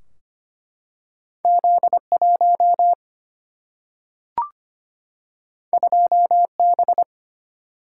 Morse code, heard as a keyed tone, about 25 wpm, 700 Hz. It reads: Z 1 2 B
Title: Single-Letter Single-Number 25wpm